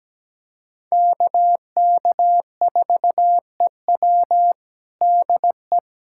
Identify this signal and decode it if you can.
Morse code, heard as a keyed tone, about 17 wpm, 700 Hz. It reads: KK4EW DE
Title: USA Callsign DE Callsign - 5 Characters - 17wpm